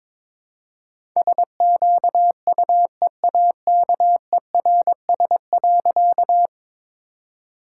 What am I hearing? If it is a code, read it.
Morse code, heard as a keyed tone, about 22 wpm, 700 Hz. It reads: SQUEAKERS.
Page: morse-code-ninja-practice-22wpm